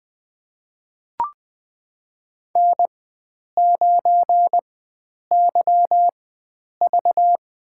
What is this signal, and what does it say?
Morse code, heard as a keyed tone, about 20 wpm, spaced as for 9 wpm, 700 Hz. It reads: N9YV
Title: USA Call Signs - 4 Characters - 4x Character Spacing - 20wpm